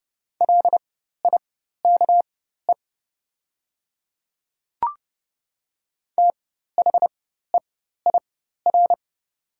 Morse code, heard as a keyed tone, about 30 wpm, spaced as for 14 wpm, 700 Hz. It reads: LIKE THEIR
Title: 4x Character Spacing - Top 300 Words - 30wpm